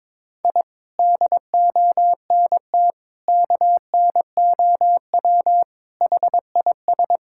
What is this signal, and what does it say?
Morse code, heard as a keyed tone, about 22 wpm, 700 Hz. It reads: I DONT KNOW HIS
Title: Sentences from Top 1000 Words - Review Part 6 - 22wpm